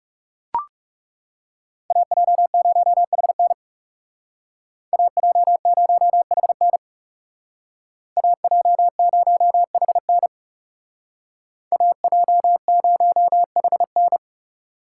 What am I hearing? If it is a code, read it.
Morse code, heard as a keyed tone, about 35 wpm, 700 Hz. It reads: AJ0HN AJ0HN AJ0HN AJ0HN
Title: Speed Racing - USA 2x2 Call Signs 30wpm